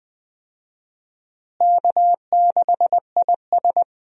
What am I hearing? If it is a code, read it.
Morse code, heard as a keyed tone, about 20 wpm, 700 Hz. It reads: K6IS
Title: USA Call Signs - 4 Characters - Repeated 3x - 20wpm